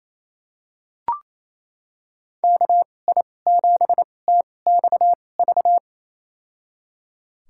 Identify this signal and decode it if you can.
Morse code, heard as a keyed tone, about 28 wpm, spaced as for 20 wpm, 700 Hz. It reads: KI7TXV
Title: USA Call Signs - 6 Characters - 2x Character Spacing - 28wpm